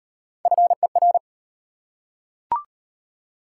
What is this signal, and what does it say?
Morse code, heard as a keyed tone, about 33 wpm, 690 Hz. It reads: FER